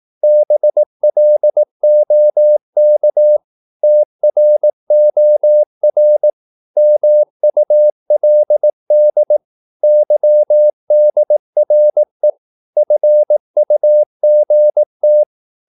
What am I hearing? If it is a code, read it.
Morse code, heard as a keyed tone, about 18 wpm, 600 Hz. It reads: BLOK TROR MULD YDRE FUGT
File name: Korte ord DK 18wpm